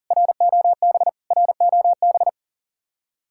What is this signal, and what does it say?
Morse code, heard as a keyed tone, about 50 wpm, 700 Hz. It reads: ROB ROB